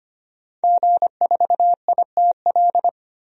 Morse code, heard as a keyed tone, about 25 wpm, 700 Hz. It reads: G4ITL
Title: DXCC-100 Call Signs - 5 Characters - 25wpm